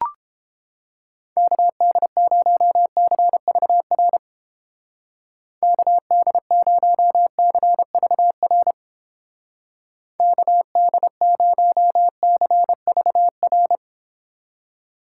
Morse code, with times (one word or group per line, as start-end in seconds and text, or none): 1.4-4.2 KD0CVR
5.6-8.7 KD0CVR
10.2-13.8 KD0CVR